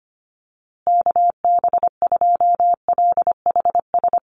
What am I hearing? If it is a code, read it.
Morse code, heard as a keyed tone, about 25 wpm, 700 Hz. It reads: KB2LHS
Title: USA Call Signs Repeated 3x 25wpm